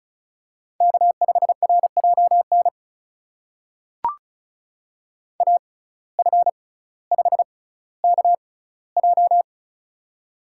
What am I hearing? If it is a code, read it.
Morse code, heard as a keyed tone, about 34 wpm, spaced as for 18 wpm, 700 Hz. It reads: K5RJN AF5KJ
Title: USA Call Signs - 5 Characters - 6x Character Spacing - 35wpm